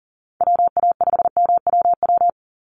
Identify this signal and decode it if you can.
Morse code, heard as a keyed tone, about 40 wpm, 700 Hz. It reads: WA5MWW
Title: USA Call Signs 40wpm